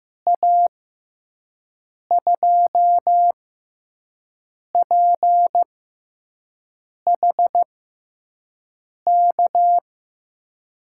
Morse code, heard as a keyed tone, about 16 wpm, spaced as for 5 wpm, 700 Hz. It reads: A2PHK
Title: USA Call Signs - 6 Characters - 6x Character Spacing - 15wpm